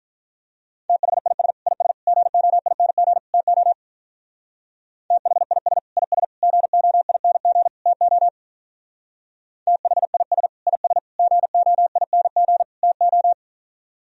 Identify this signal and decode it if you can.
Morse code, heard as a keyed tone, about 45 wpm, 700 Hz. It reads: THIS IS GOING TO THIS IS GOING TO THIS IS GOING TO